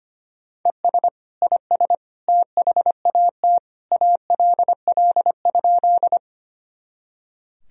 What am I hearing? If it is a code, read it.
Morse code, heard as a keyed tone, about 25 wpm, 700 Hz. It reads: ES IS THAT ALL?
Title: Sentences from Top 100 Words and Abbreviations 25wpm